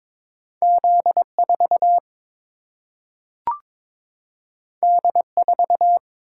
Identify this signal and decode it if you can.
Morse code, heard as a keyed tone, about 22 wpm, 700 Hz. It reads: Z 4 D 4
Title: Single-Letter Single-Number 22wpm